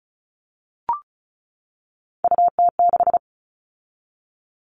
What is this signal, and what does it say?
Morse code, heard as a keyed tone, about 34 wpm, 700 Hz. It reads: UT6
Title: All Cut Numbers - 3-Digit 35wpm